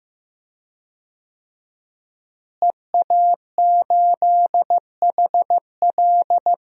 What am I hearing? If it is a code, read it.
Morse code, heard as a keyed tone, about 15 wpm, 700 Hz. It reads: EA8HL
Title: DXCC-100 Call Signs - 5 Characters - Repeated 2x - 15wpm